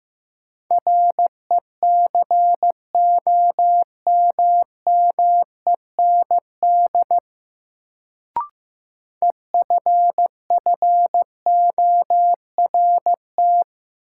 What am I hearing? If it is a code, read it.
Morse code, heard as a keyed tone, about 15 wpm, 700 Hz. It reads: RECOMMEND EFFORT
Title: Sentences from Top 1200 Words - New Word Drill Alt - 15wpm